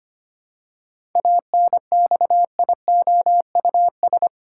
Morse code, heard as a keyed tone, about 25 wpm, 700 Hz. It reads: ANXIOUS
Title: Seven Letter Words - 25wpm